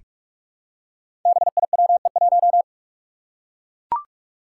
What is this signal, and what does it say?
Morse code, heard as a keyed tone, about 44 wpm, 700 Hz. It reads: BIWE1
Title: Five Challenging Letter-Number Combinations 45wpm